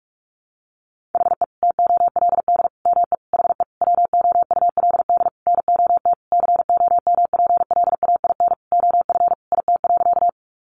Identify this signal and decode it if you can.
Morse code, heard as a keyed tone, about 44 wpm, 700 Hz. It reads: HE TOLD ME HE WOULD NOT COMPLAIN OF IT.
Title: Sentences from Top 1300 Words 45wpm